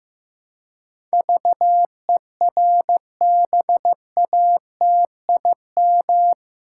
Morse code, heard as a keyed tone, about 15 wpm, 700 Hz. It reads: VERBATIM
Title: Pangram Challenge v2 - 15wpm